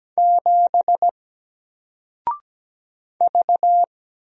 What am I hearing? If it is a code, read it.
Morse code, heard as a keyed tone, about 17 wpm, 700 Hz. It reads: 7 V
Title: Single Character (Letter:Number:Punctuation:Prosign) 17wpm